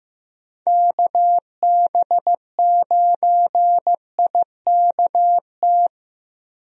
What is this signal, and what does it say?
Morse code, heard as a keyed tone, about 15 wpm, 700 Hz. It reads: KB9IKT
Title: USA Call Signs - 6 Characters - Repeated 3x - 15wpm